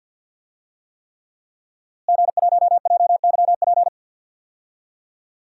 Morse code, heard as a keyed tone, about 47 wpm, 700 Hz. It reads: G1JYP
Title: DXCC-100 Call Signs - 5 Characters - Repeated 3x - 50wpm